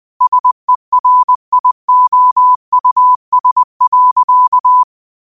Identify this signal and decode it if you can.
Morse code, heard as a keyed tone, about 20 wpm, 1000 Hz. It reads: SERIOUS.
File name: qotdfiles / 20wpm